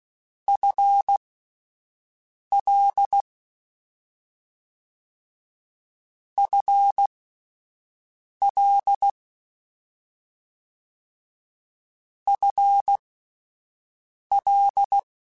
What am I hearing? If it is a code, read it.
Morse code, heard as a keyed tone, about 16 wpm, spaced as for 5 wpm, 800 Hz. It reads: FL FL FL
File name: Combinations / Combinations-mp3s / Combinations@5